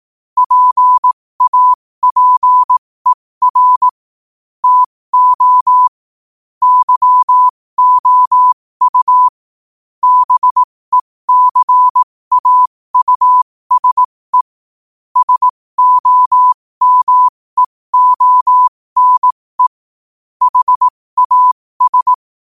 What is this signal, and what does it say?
Morse code, heard as a keyed tone, about 18 wpm, spaced as for 15 wpm, 1000 Hz. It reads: PAPER TO YOU BECAUSE SOMEONE HAS
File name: qotdfiles / 15wpm